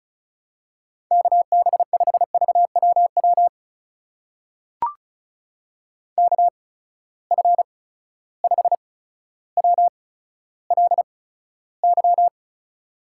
Morse code, heard as a keyed tone, about 36 wpm, spaced as for 15 wpm, 700 Hz. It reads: KB5VWW KF5WLY
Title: USA Call Signs - 6 Characters - 8x Character Spacing - 35wpm